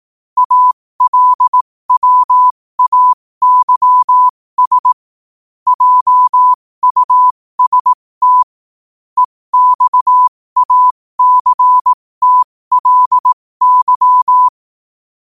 Morse code, heard as a keyed tone, about 18 wpm, spaced as for 15 wpm, 1000 Hz. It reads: ALWAYS JUST EXACTLY
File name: qotdfiles / 15wpm